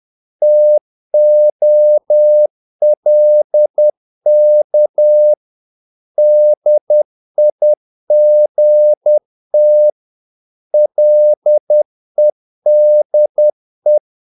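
Morse code, heard as a keyed tone, about 10 wpm, 600 Hz. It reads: TOLK DIGT LEDE
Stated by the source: Morse practice of short Danish words at 10 wpm, from CW med Gnister.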